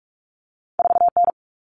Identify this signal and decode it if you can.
Morse code, heard as a keyed tone, about 46 wpm, 700 Hz. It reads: 4N
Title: Common Cut Numbers - 2-Digit 45wpm